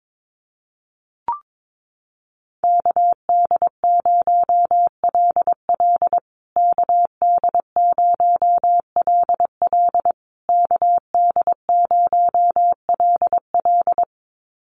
Morse code, heard as a keyed tone, about 22 wpm, 700 Hz. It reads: KD0LL KD0LL KD0LL
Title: USA Call Signs Repeated 3x 22wpm